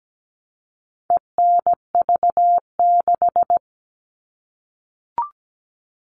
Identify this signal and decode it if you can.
Morse code, heard as a keyed tone, about 17 wpm, 700 Hz. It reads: ENV6
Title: All Cut Numbers - 4-Digit 17wpm